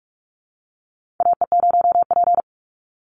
Morse code, heard as a keyed tone, about 46 wpm, 700 Hz. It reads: AE0P
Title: Call Signs - 45wpm